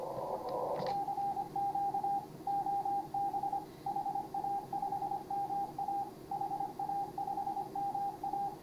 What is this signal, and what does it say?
Morse code, heard as a keyed tone, about 24 wpm, 810 Hz, with radio noise. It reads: CQ OB HA5MA HA5MA